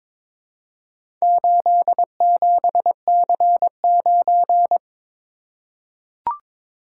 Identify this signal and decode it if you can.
Morse code, heard as a keyed tone, about 22 wpm, 700 Hz. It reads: 87C9
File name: practice / 22wpm / Four Letter-Number Combinations - 22wpm